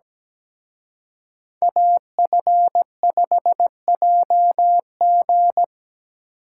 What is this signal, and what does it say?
Morse code, heard as a keyed tone, about 20 wpm, 700 Hz. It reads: AF5JG
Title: Speed Racing - USA 2x2 Call Signs 17wpm